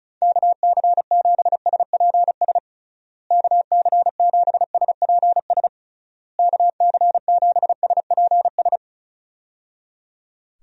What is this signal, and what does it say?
Morse code, heard as a keyed tone, about 34 wpm, 700 Hz. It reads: KC7SPS KC7SPS KC7SPS
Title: USA Call Signs - 6 Characters - Repeated 3x - 35wpm